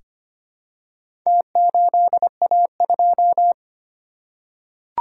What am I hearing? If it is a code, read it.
Morse code, heard as a keyed tone, about 25 wpm, 700 Hz. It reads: T8A2
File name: practice / 25wpm / Four Letter-Number Combinations - 25wpm